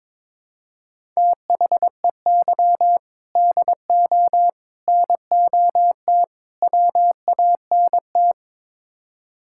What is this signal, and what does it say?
Morse code, heard as a keyed tone, about 22 wpm, 700 Hz. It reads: THEY DO NOT WANT
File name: practice / 22wpm / Build A Phrase - 5-Word Phrases - 22wpm